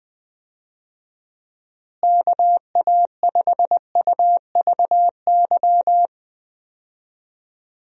Morse code, heard as a keyed tone, about 20 wpm, 700 Hz. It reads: KA5UVY
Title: USA Call Signs - 6 Characters - Repeated 2x - 20wpm